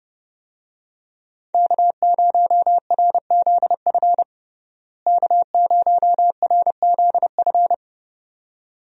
Morse code, heard as a keyed tone, about 32 wpm, 700 Hz. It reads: K0RZF K0RZF
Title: USA Call Signs - 5 Characters - Repeated 2x - 30wpm